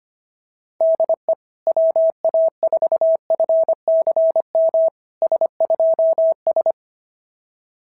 Morse code, heard as a keyed tone, about 25 wpm, 690 Hz. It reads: DE WA4FCM S2S
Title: SOTA QSO Practice 25wpm